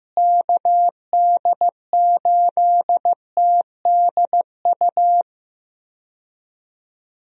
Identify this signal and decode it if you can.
Morse code, heard as a keyed tone, about 15 wpm, 700 Hz. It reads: KD8TDU
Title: USA Call Signs - 6 Characters - Repeated 3x - 15wpm